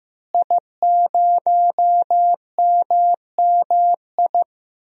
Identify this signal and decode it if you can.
Morse code, heard as a keyed tone, about 15 wpm, 700 Hz. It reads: I0MMI
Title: DXCC-100 Call Signs - 5 Characters - Repeated 3x - 15wpm